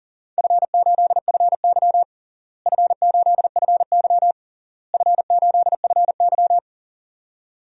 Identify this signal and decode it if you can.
Morse code, heard as a keyed tone, about 40 wpm, 700 Hz. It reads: F8FY F8FY F8FY
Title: DXCC-100 Call Signs - 4 Characters - Repeated 3x - 40wpm